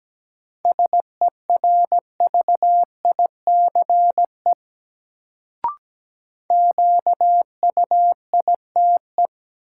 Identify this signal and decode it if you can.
Morse code, heard as a keyed tone, about 17 wpm, 700 Hz. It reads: SERVICE QUITE
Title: Sentences from Top 700 Words - New Word Drill - 17wpm